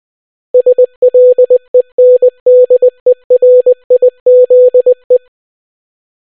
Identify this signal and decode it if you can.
Morse code, heard as a keyed tone, about 20 wpm, 500 Hz. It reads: SLENDERIZE